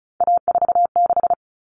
Morse code, heard as a keyed tone, about 34 wpm, 700 Hz. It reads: A46
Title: Common Cut Numbers - 3-Digit 35wpm